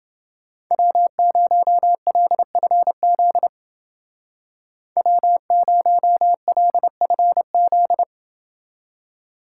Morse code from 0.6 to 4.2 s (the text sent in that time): W0LFZ